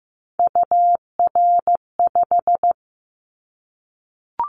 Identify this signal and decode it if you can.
Morse code, heard as a keyed tone, about 15 wpm, 700 Hz. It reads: UR5
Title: Callsign Prefixes for Top 25 DXCC - Repeated 3x - 15wpm